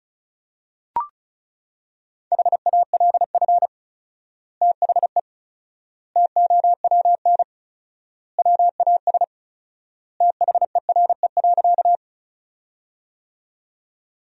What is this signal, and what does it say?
Morse code, heard as a keyed tone, about 34 wpm, 700 Hz. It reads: HALF THE TOWN WAS THERE.
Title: 4x Word-Spacing - Sentences from Top 500 Words - 35wpm